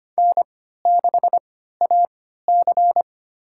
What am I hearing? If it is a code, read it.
Morse code, heard as a keyed tone, about 25 wpm, spaced as for 14 wpm, 700 Hz. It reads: N6AC
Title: USA Call Signs - 4 Characters - 3x Character Spacing - 25wpm